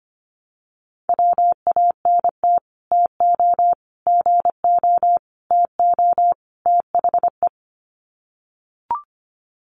Morse code, heard as a keyed tone, about 25 wpm, 700 Hz. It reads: WANT TO GO TO THE
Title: Sets of 5 Words Encore2 25wpm